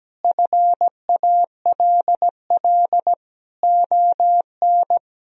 Morse code, heard as a keyed tone, about 17 wpm, 700 Hz. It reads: FALL ON
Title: Sentences from Top 1000 Words - Review Part 2 - 17wpm